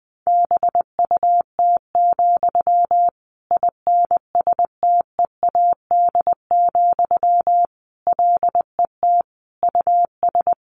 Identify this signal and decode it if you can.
Morse code, heard as a keyed tone, about 20 wpm, 700 Hz. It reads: BUT, INSTEAD, LET US
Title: Sentences from Top 1500 Words 20wpm